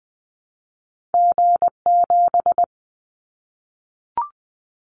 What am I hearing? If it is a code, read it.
Morse code, heard as a keyed tone, about 20 wpm, 700 Hz. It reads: G7
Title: Callsign Prefixes for Top 25 DXCC - Repeated 3x - 20wpm